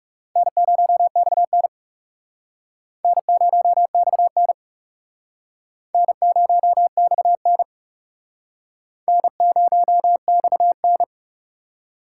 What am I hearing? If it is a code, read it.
Morse code, heard as a keyed tone, about 40 wpm, 700 Hz. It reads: N0XN N0XN N0XN N0XN